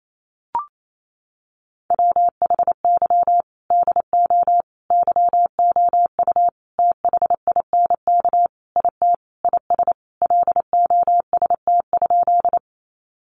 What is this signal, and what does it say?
Morse code, heard as a keyed tone, about 28 wpm, 700 Hz. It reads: WHY DO YOU THINK IT IS LOST?
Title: Sentences from Top 1400 Words 28wpm